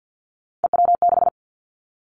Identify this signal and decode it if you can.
Morse code, heard as a keyed tone, about 50 wpm, 700 Hz. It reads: EW6
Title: Callsign Prefixes for Top 100 DXCC - 50wpm